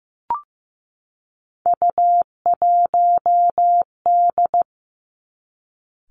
Morse code, heard as a keyed tone, about 15 wpm, 700 Hz. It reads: U1D
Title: Three Challenging Letter-Number Combinations 15wpm